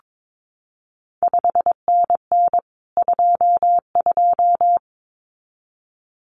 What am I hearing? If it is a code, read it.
Morse code, heard as a keyed tone, about 22 wpm, 700 Hz. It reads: 5NN 22
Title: CQ WW DX Contest Exchange Encore - 22wpm